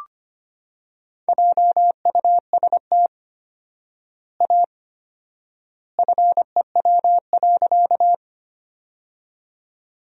Morse code, heard as a keyed tone, about 25 wpm, 700 Hz. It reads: JUST A FEW.
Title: 4x Word-Spacing - Sentences from Top 300 Words - 25wpm